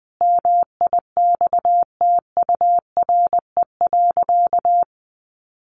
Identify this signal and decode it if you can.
Morse code, heard as a keyed tone, about 20 wpm, 700 Hz. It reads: MIXTURE.
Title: Sentences from Top 1500 Words 20wpm